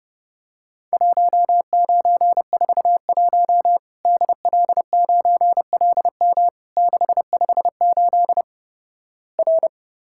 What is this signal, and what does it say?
Morse code, heard as a keyed tone, about 30 wpm, 700 Hz. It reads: 1941 DL9LM 658 R
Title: Worked All Europe DX Contest - QTC Exchange - 30wpm